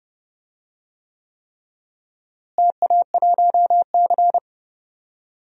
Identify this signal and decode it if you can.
Morse code, heard as a keyed tone, about 30 wpm, 700 Hz. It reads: TA1C
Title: DXCC-100 Call Signs - 4 Characters - Repeated 3x - 30wpm